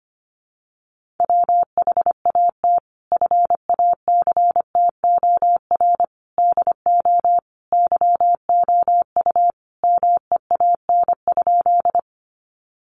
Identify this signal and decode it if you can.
Morse code, heard as a keyed tone, about 25 wpm, 700 Hz. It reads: WHAT FACTOR DO YOU MEAN?
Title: Sentences from Top 1300 Words 25wpm